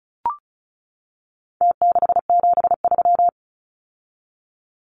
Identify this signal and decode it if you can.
Morse code, heard as a keyed tone, about 34 wpm, 700 Hz. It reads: T673
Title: Numbers - Contest Style - 4-Digit 35wpm